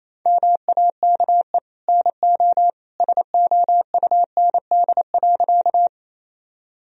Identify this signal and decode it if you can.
Morse code, heard as a keyed tone, about 28 wpm, 700 Hz. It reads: MAKE NO SOUND.
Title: Sentences from Top 1000 Words - Review Part 1 - 28wpm